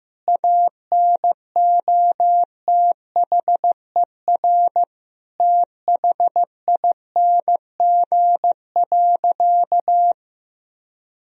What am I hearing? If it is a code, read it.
Morse code, heard as a keyed tone, about 15 wpm, 700 Hz. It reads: ANOTHER THING.
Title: Sentences from Top 1000 Words - Review Part 3 - 15wpm